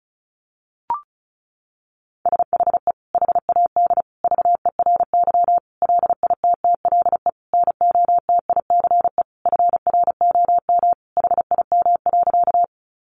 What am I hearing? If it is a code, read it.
Morse code, heard as a keyed tone, about 34 wpm, 700 Hz. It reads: SHE HAD VERY LITTLE NOTICE FROM HIM.
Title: Sentences from Top 400 Words 35wpm